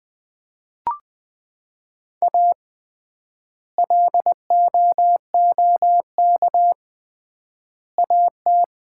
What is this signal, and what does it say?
Morse code, heard as a keyed tone, about 20 wpm, 700 Hz. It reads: A LOOK AT
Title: 3x Word-Spacing - Sets of 3 Words - 20wpm